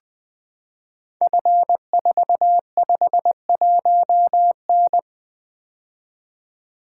Morse code, heard as a keyed tone, about 20 wpm, 700 Hz. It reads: F451N
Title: Code Groups - 5 Letter:Numbers - 20wpm